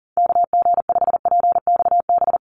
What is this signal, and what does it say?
Morse code, heard as a keyed tone, about 40 wpm, 700 Hz. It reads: KG5PXB
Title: USA Call Signs Repeated 2x 40wpm